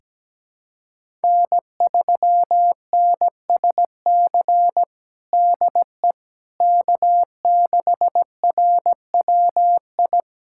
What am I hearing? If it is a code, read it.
Morse code, heard as a keyed tone, about 17 wpm, 700 Hz. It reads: N3NSC DE K6RWI
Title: USA Callsign DE Callsign - 5 Characters - 17wpm